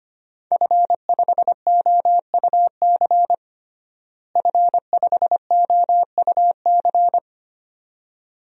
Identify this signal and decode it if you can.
Morse code, heard as a keyed tone, about 25 wpm, 700 Hz. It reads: F5OUC F5OUC
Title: DXCC-100 Call Signs - 5 Characters - Repeated 2x - 25wpm